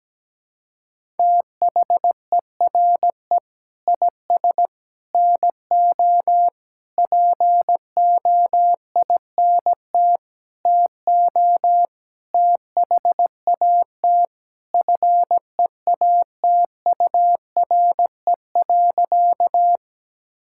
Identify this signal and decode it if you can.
Morse code, heard as a keyed tone, about 17 wpm, 700 Hz. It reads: THERE IS NO POINT TO THAT FEATURE.
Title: Sentences from Top 1000 Words - Review Part 5 - 17wpm